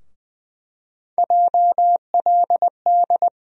Morse code, heard as a keyed tone, about 20 wpm, 700 Hz. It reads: JLD
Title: USA Call Sign Postfixes - 3 Letters 20wpm